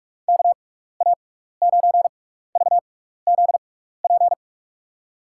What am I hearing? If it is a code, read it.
Morse code, heard as a keyed tone, about 47 wpm, spaced as for 16 wpm, 700 Hz. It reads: KA9VZP
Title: USA Call Signs - 6 Characters - 6x Character Spacing - 45wpm